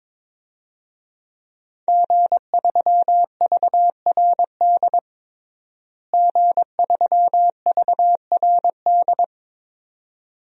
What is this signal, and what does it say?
Morse code, heard as a keyed tone, about 22 wpm, 700 Hz. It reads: G3VRD G3VRD
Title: DXCC-100 Call Signs - 5 Characters - Repeated 2x - 22wpm